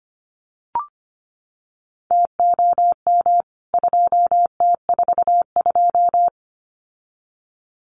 Morse code, heard as a keyed tone, about 25 wpm, 700 Hz. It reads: TOM 2T42
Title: CWT Exchange 25wpm